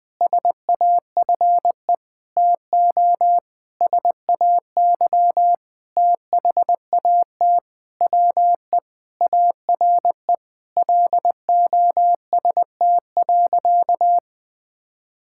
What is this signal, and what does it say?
Morse code, heard as a keyed tone, about 21 wpm, 700 Hz. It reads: SAFE TO SAY THAT WE ARE LOST.
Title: Sentences with 7-Word Phrases - 20wpm